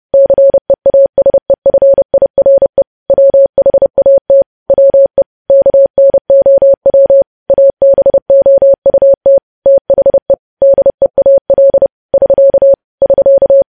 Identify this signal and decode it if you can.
Morse code, heard as a keyed tone, about 30 wpm, 550 Hz. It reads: CEASEFIRE WHAT WE KNOW ABOUT THE DEAL <SK> <SK>
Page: news headlines in morse code at 30 wpm